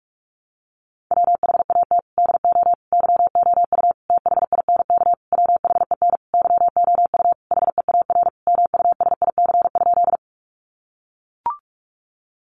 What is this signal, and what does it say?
Morse code, heard as a keyed tone, about 44 wpm, 700 Hz. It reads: WHAT DO YOU THINK WHEN YOU HEAR MUSIC?
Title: Sentences from Top 300 Words 45wpm